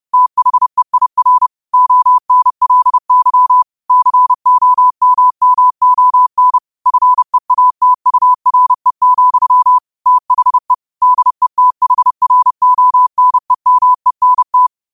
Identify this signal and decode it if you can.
Morse code, heard as a keyed tone, about 30 wpm, 1000 Hz. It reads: THEIR ONLY COMMON FEATURE, THE DETHRONEMENT
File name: qotdfiles / 30wpm